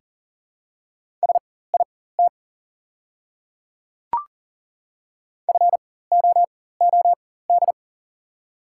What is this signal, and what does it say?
Morse code, heard as a keyed tone, about 40 wpm, spaced as for 19 wpm, 690 Hz. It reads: SIT FOOD
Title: 4x Character Spacing - Top 500 Words - 40wpm